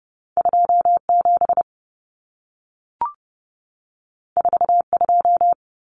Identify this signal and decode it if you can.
Morse code, heard as a keyed tone, about 30 wpm, 700 Hz. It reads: 27 42
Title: Common Cut Numbers - 2-Digit 30wpm